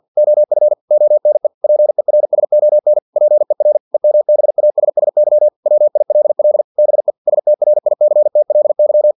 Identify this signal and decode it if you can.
Morse code, heard as a keyed tone, about 50 wpm, 600 Hz. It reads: OF ONE PERSON PER EMBASSY WILL BE STRICTLY
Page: Never 49wpm